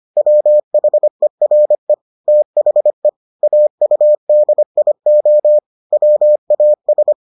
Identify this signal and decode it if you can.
Morse code, heard as a keyed tone, about 25 wpm, 600 Hz. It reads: WHERE THE AUDIO WAS